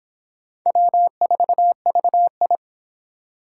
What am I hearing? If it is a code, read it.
Morse code, heard as a keyed tone, about 26 wpm, 700 Hz. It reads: W4VI